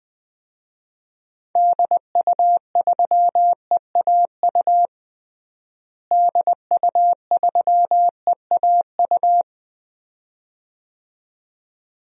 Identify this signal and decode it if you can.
Morse code, heard as a keyed tone, about 20 wpm, 700 Hz. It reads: DU3EAU DU3EAU
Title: DXCC-100 Call Signs - 6 Characters - Repeated 2x - 20wpm